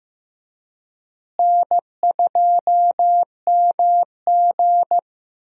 Morse code, heard as a keyed tone, about 15 wpm, 700 Hz. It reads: N2MG
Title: USA Call Signs - 4 Characters - Repeated 3x - 15wpm